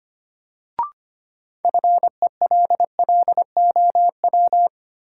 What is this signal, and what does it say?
Morse code, heard as a keyed tone, about 25 wpm, 700 Hz. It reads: FELLOW
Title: Sets of 5 Words Encore - New Word Drill - 25wpm